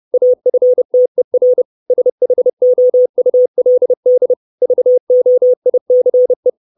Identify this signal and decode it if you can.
Morse code, heard as a keyed tone, about 30 wpm, 500 Hz. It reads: AFTER SHOULD VOICE
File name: 632 Words 30WPM